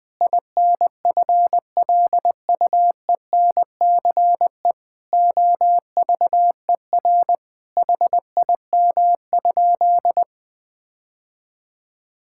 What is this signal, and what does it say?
Morse code, heard as a keyed tone, about 20 wpm, 700 Hz. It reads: INFLUENCE OVER HIM?
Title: Sentences from Top 1600 Words 20wpm